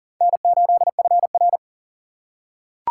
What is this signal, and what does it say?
Morse code, heard as a keyed tone, about 40 wpm, 700 Hz. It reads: N8FR
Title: USA Call Signs - 4 Characters - 2x Character Spacing - 40wpm